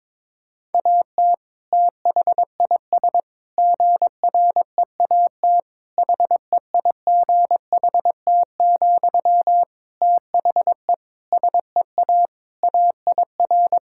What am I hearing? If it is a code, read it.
Morse code, heard as a keyed tone, about 22 wpm, 700 Hz. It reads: AT THIS GREAT HEIGHT, THE SEA AIR
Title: Sentences from Top 1600 Words 22wpm